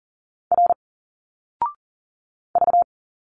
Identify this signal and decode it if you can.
Morse code, heard as a keyed tone, about 40 wpm, 700 Hz. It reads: R V
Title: Single Letters 40wpm